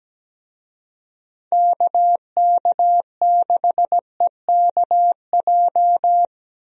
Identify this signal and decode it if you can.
Morse code, heard as a keyed tone, about 17 wpm, 700 Hz. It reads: KK6EKJ
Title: USA Call Signs - 6 Characters - Repeated 3x - 17wpm